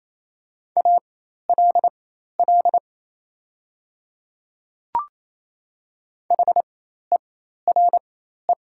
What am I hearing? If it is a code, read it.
Morse code, heard as a keyed tone, about 28 wpm, spaced as for 13 wpm, 700 Hz. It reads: ALL HERE
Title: 4x Character Spacing - Top 200 Words - 28wpm